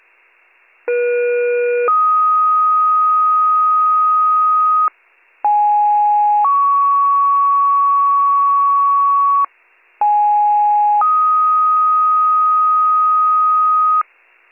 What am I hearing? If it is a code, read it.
Two-tone page, heard as a one-second tone, then a three-second tone, about 480 then 1250 Hz, about 830 then 1120 Hz, about 830 then 1250 Hz. It is heard from 0.9 to 14.0 s.